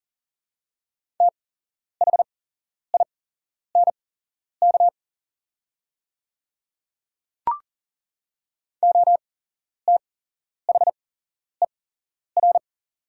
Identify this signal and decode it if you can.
Morse code, heard as a keyed tone, about 38 wpm, spaced as for 11 wpm, 700 Hz. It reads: THINK OTHER
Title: 8x Character Spacing - Top 300 Words - 40wpm